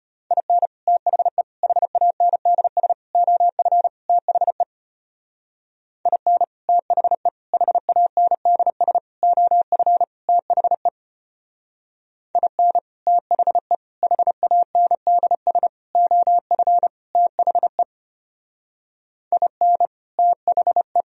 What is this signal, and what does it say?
Morse code, heard as a keyed tone, about 31 wpm, 700 Hz. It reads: IN THE HANDS OF THE IN THE HANDS OF THE IN THE HANDS OF THE IN THE
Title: Speed-Racing - Sets of 5 Words 25wpm